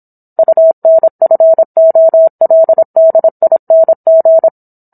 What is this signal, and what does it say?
Morse code, heard as a keyed tone, about 27 wpm, 650 Hz. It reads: UNFOLDING